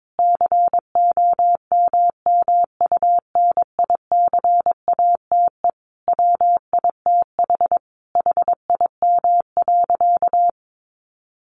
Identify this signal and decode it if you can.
Morse code, heard as a keyed tone, about 22 wpm, 700 Hz. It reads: COMMUNICATE WITH HIM.
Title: Sentences from Top 1300 Words 22wpm